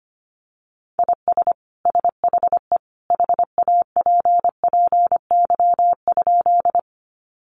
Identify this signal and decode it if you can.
Morse code, heard as a keyed tone, about 25 wpm, 700 Hz. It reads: IS SHE HAPPY?
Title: Sentences from Top 1000 Words 25wpm